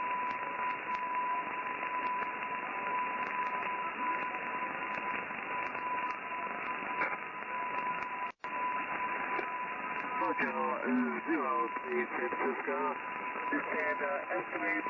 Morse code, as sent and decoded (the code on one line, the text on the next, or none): none